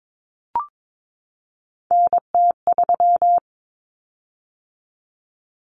Morse code, heard as a keyed tone, about 22 wpm, 700 Hz. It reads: NT3
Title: Common Cut Numbers - 3-Digit 22wpm